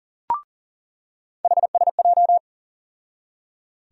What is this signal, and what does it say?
Morse code, heard as a keyed tone, about 39 wpm, 700 Hz. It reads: HSJ
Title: USA Call Sign Postfixes - 3 Letters 40wpm